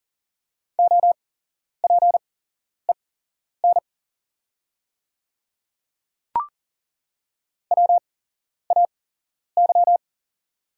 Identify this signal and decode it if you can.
Morse code, heard as a keyed tone, about 41 wpm, spaced as for 11 wpm, 690 Hz. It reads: OPEN WAY
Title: 8x Character Spacing - Top 500 Words - 40wpm